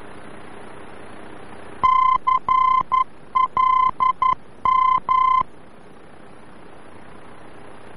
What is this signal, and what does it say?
Morse code, heard as a keyed tone, about 11 wpm, 1030 Hz. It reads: CLM
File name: france / mx / CLM-VOR-F